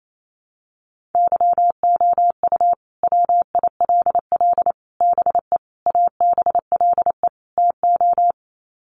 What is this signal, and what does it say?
Morse code, heard as a keyed tone, about 28 wpm, 700 Hz. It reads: YOU WILL BE ABLE TO
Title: Sets of 5 Words Encore2 28wpm